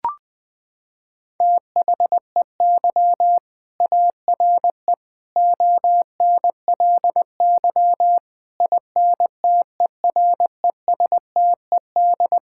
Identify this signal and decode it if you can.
Morse code, heard as a keyed tone, about 20 wpm, 700 Hz. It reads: THEY ARE ONLY INTERESTED
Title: Sentences from Top 1000 Words - Review Part 6 - 20wpm